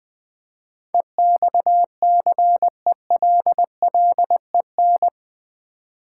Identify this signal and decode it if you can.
Morse code, heard as a keyed tone, about 20 wpm, 700 Hz. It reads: EXCELLEN
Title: Build A Word - 4 Letters At A Time - 10-Letter Words - 20wpm